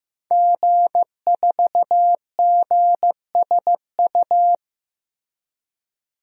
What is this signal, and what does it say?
Morse code, heard as a keyed tone, about 15 wpm, 700 Hz. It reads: G4GSU
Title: DXCC-100 Call Signs - 5 Characters - Repeated 3x - 15wpm